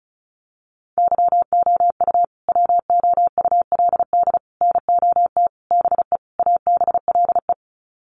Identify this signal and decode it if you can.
Morse code, heard as a keyed tone, about 34 wpm, 700 Hz. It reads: YOU WOULD NOT BE ABLE
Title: Sets of 5 Words Encore2 35wpm